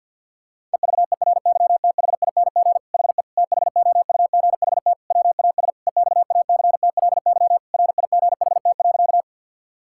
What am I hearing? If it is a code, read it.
Morse code, heard as a keyed tone, about 50 wpm, 700 Hz. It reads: EVERYTHING HE THOUGHT WAS EXACTLY RIGHT.
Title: Sentences from Top 1000 Words - Review Part 6 - 50wpm